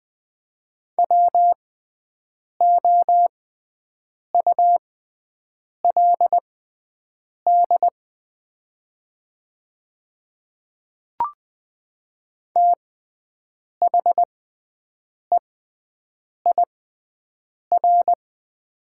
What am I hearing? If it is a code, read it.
Morse code, heard as a keyed tone, about 20 wpm, spaced as for 7 wpm, 700 Hz. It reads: WOULD THEIR
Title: 6x Character Spacing - Top 300 Words - 20wpm